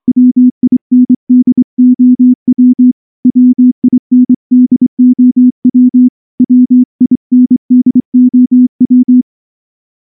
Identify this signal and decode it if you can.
Morse code, heard as a keyed tone, about 23 wpm, 250 Hz. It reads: WINDOW WINDOW WINDOW
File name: morse-training-25WPM